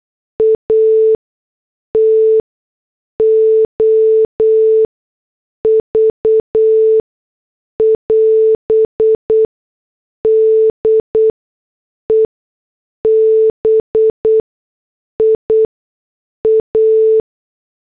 Morse code, heard as a keyed tone, about 8 wpm, spaced as for 6 wpm, 440 Hz. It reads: ATOV <AS> DEBIA
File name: morse / headlines